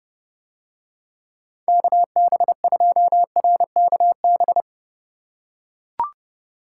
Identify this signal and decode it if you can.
Morse code, heard as a keyed tone, about 27 wpm, 700 Hz. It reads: KB2RKB